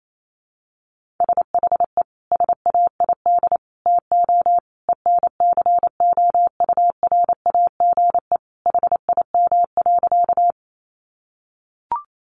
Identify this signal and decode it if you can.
Morse code, heard as a keyed tone, about 28 wpm, 700 Hz. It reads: SHE SAID TO ENCOURAGE HIM.
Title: Sentences from Top 1400 Words 28wpm